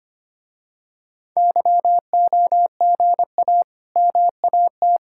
Morse code, heard as a keyed tone, about 25 wpm, 700 Hz. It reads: YOGA MAT
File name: practice / 25wpm / Difficult Words with Hints - 25wpm